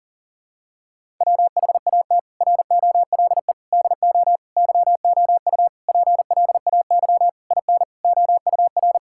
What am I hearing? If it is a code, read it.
Morse code, heard as a keyed tone, about 40 wpm, 700 Hz. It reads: WHAT ROLE DO YOU PLAY IN OUR
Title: Sentences from Top 1000 Words - Review Part 6 - 40wpm